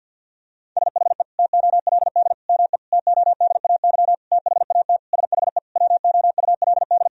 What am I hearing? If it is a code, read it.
Morse code, heard as a keyed tone, about 50 wpm, 700 Hz. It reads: SHE TOLD ME TODAY THAT SHE WOULD
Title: Sentences from Top 1000 Words - Review Part 4 - 50wpm